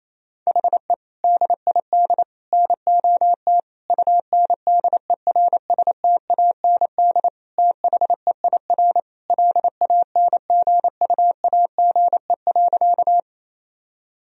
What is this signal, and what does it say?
Morse code, heard as a keyed tone, about 28 wpm, 700 Hz. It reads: HE DID NOT UNDERSTAND THEIR LANGUAGE.
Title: Sentences from Top 1000 Words - Review Part 3 - 28wpm